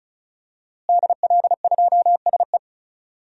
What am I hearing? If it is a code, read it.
Morse code, heard as a keyed tone, about 36 wpm, 700 Hz. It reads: DL2SE
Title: DXCC-100 Call Signs - 5 Characters - Repeated 2x - 35wpm